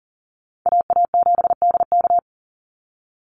Morse code, heard as a keyed tone, about 40 wpm, 700 Hz. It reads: AA7DK
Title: NA SOTA Chaser Callsigns 40wpm